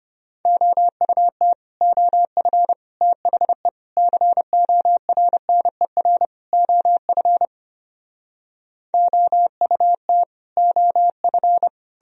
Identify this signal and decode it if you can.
Morse code, heard as a keyed tone, about 33 wpm, 700 Hz. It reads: OUT OF THE CORNER OF OUT OF